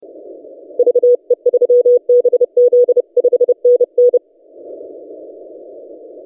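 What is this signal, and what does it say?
Morse code, heard as a keyed tone, about 30 wpm, 490 Hz. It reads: VE3BZ 5NN
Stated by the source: DX station 3D2GC on 20 metres CW, recorded by N6ML.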